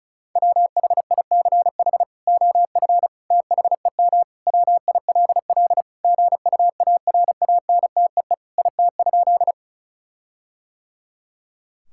Morse code, heard as a keyed tone, about 34 wpm, 700 Hz. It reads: WHICH OF THEM WILL GUARANTEE IT?
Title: Sentences from Top 1600 Words 35wpm